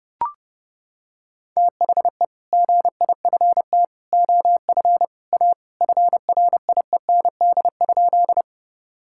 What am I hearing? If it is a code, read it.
Morse code, heard as a keyed tone, about 30 wpm, 700 Hz. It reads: THE GIFT OF A FRIEND?
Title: Sentences from Top 1000 Words - Review Part 6 - 30wpm